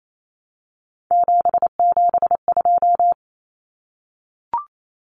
Morse code, heard as a keyed tone, about 28 wpm, 700 Hz. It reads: 772
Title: Numbers - 3-Digit 28wpm